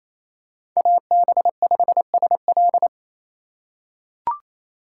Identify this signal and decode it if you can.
Morse code, heard as a keyed tone, about 28 wpm, 700 Hz. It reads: AB5SL
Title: USA Call Signs - 5 Characters - 2x Character Spacing - 28wpm